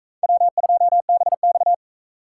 Morse code, heard as a keyed tone, about 38 wpm, 700 Hz. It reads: W2BX